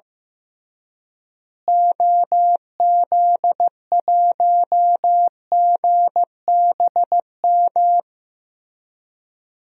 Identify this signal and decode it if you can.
Morse code, heard as a keyed tone, about 15 wpm, 700 Hz. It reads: OZ1GBM
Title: DXCC-100 Call Signs - 6 Characters - Repeated 3x - 15wpm